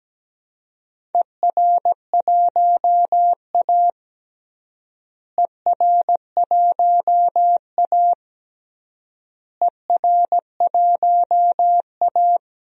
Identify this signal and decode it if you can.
Morse code, heard as a keyed tone, about 17 wpm, 700 Hz. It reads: ER1A ER1A ER1A
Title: DXCC-100 Call Signs - 4 Characters - Repeated 3x - 17wpm